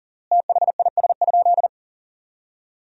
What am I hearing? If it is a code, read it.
Morse code, heard as a keyed tone, about 40 wpm, 700 Hz. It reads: THIS?
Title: 3x Word-Spacing - Sentences from Top 100 Words - 40wpm